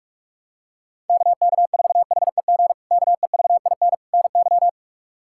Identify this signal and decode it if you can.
Morse code, heard as a keyed tone, about 41 wpm, 700 Hz. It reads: KK4HEG KEVIN NY
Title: K1USN SST QSO Practice - 45wpm